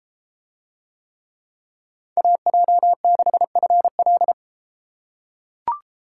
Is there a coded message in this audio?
Yes: Morse code from 2.2 to 4.3 s, heard as a keyed tone, about 32 wpm, 700 Hz. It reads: AJ6FL